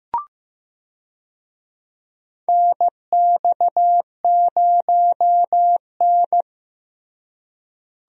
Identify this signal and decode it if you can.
Morse code, heard as a keyed tone, about 15 wpm, 700 Hz. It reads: NX0N
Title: USA Call Signs - 4 Characters - Repeated 2x - 15wpm